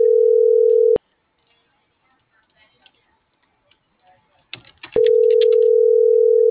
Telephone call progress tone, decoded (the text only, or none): ringback tone